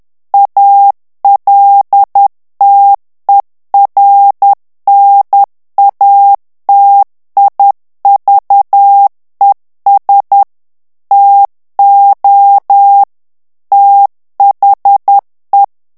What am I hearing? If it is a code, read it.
Morse code, heard as a keyed tone, about 10 wpm, 800 Hz. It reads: ALTERNATIVES TO THE
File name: ANGLAIS texte clair 3 vitesse 10 mots minute